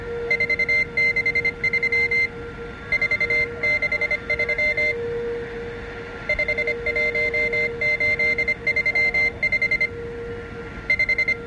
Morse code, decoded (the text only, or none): 463 463 51835 5